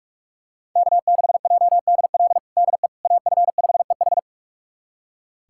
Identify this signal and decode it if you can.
Morse code, heard as a keyed tone, about 41 wpm, 700 Hz. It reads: K6JDL DE AF5EH
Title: USA Callsign DE Callsign - 5 Characters - 45wpm